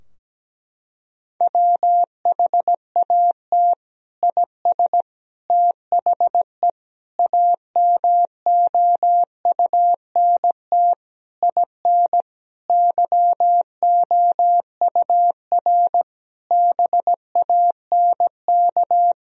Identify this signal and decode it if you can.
Morse code, heard as a keyed tone, about 17 wpm, 700 Hz. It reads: WHAT IS THE AMOUNT IN YOUR BANK
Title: Sentences from Top 1000 Words - Review Part 4 - 17wpm